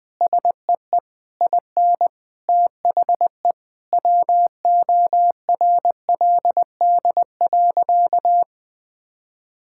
Morse code, heard as a keyed tone, about 20 wpm, 700 Hz. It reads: SEE IN THE WORLD.
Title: Sentences with 9-Word Phrases - 20wpm